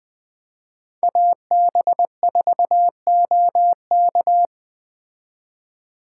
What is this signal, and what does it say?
Morse code, heard as a keyed tone, about 19 wpm, 700 Hz. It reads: AB4OK